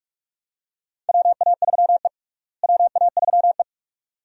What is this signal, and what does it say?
Morse code, heard as a keyed tone, about 46 wpm, 700 Hz. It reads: WA3E WA3E
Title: USA Call Signs - 4 Characters - Repeated 2x - 45wpm